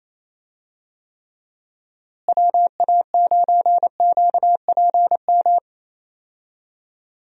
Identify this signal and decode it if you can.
Morse code, heard as a keyed tone, about 27 wpm, 700 Hz. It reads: WA9QPM
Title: USA Call Signs - 6 Characters - Repeated 3x - 28wpm